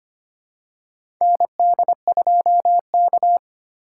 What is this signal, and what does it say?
Morse code, heard as a keyed tone, about 25 wpm, 700 Hz. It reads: ND2K
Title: USA Call Signs - 2x Character Spacing - 25wpm